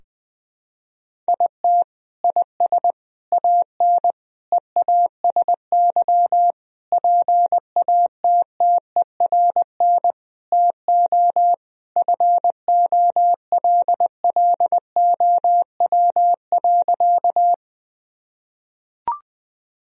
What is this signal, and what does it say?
Morse code, heard as a keyed tone, about 20 wpm, 700 Hz. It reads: IT IS AN EASY PATTERN TO FOLLOW.
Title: Sentences from Top 1000 Words - Review Part 2 - 20wpm